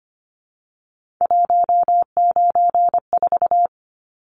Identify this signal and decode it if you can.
Morse code, heard as a keyed tone, about 25 wpm, 700 Hz. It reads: 194
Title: Numbers - 3-Digit 25wpm